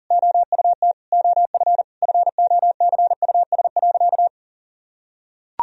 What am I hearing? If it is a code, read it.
Morse code, heard as a keyed tone, about 37 wpm, 700 Hz. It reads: OUT OF FOCUS.
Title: Sentences from Top 1000 Words - Review Part 4 - 40wpm